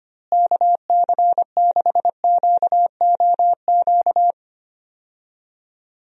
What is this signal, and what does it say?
Morse code, heard as a keyed tone, about 25 wpm, 700 Hz. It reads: KC6QOQ
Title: USA Call Signs - 6 Characters - Repeated 3x - 25wpm